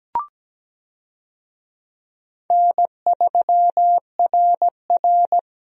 Morse code, heard as a keyed tone, about 17 wpm, 700 Hz. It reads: N3RR
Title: USA Call Signs - 4 Characters - Repeated 3x - 17wpm